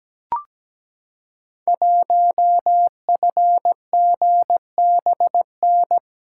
Morse code, heard as a keyed tone, about 17 wpm, 700 Hz. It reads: 1FGBN
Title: Code Groups - 5 Letter:Numbers - 17wpm